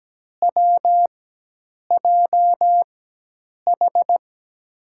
Morse code, heard as a keyed tone, about 17 wpm, spaced as for 8 wpm, 700 Hz. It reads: WJH
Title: USA Call Signs - 4x Character Spacing - 17wpm